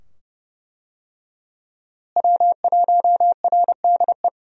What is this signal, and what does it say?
Morse code, heard as a keyed tone, about 31 wpm, 700 Hz. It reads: W1RDE
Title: USA Call Signs - 5 Characters - Repeated 2x - 30wpm